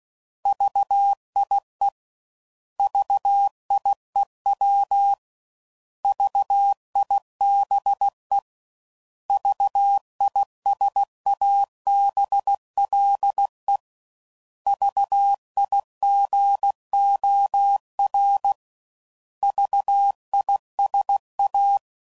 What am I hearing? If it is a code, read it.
Morse code, heard as a keyed tone, about 16 wpm, 800 Hz. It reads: VIE VIEW VIBE VISABLE VIGOR VISA
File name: Prefixes / By First Letter / Vs